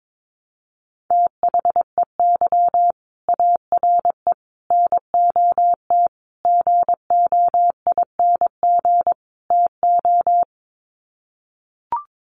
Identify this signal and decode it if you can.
Morse code, heard as a keyed tone, about 22 wpm, 700 Hz. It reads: THEY ARE NOT GOING TO
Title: Sets of 5 Words Encore2 22wpm